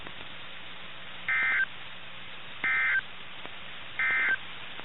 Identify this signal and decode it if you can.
Emergency Alert System (SAME) tones, heard as harsh buzzing bursts. It is heard from 1.2 to 4.4 s.
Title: NOAA Weather Radio - KIH61